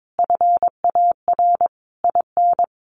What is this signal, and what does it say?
Morse code, heard as a keyed tone, about 22 wpm, 700 Hz. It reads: FAR IN
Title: Sentences from Top 1400 Words 22wpm